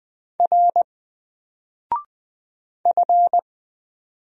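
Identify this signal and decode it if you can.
Morse code, heard as a keyed tone, about 20 wpm, 700 Hz. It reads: R F
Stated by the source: Single letter and number Morse drill at 20 wpm, from N4LCD.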